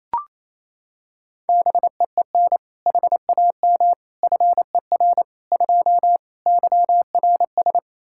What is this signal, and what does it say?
Morse code, heard as a keyed tone, about 28 wpm, 700 Hz. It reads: BEEN HAM FER 2 YRS
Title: Practice QSO - Exchange 3 - More Info - 28wpm